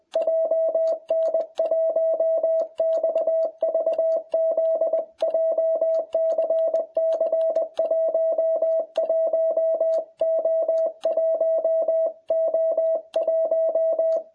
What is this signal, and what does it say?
Morse code, heard as a keyed tone, about 21 wpm, 660 Hz. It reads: JN1XVZ JCC 11O1O1